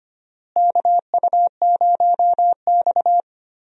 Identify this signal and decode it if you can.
Morse code, heard as a keyed tone, about 25 wpm, 700 Hz. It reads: KU0X
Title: USA Call Signs - 4 Characters - Repeated 2x - 25wpm